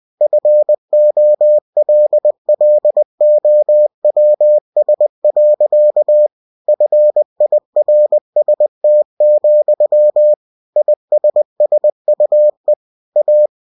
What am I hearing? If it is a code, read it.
Morse code, heard as a keyed tone, about 20 wpm, 600 Hz. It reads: FOLLOWS. FIRST, ISSUE A
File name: cw-books / winnie-the-pooh / winnie-20wpm-page-0030